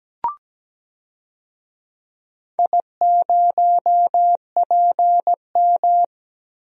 Morse code, heard as a keyed tone, about 17 wpm, 700 Hz. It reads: I0PM
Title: DXCC-100 Call Signs - 4 Characters - Repeated 3x - 17wpm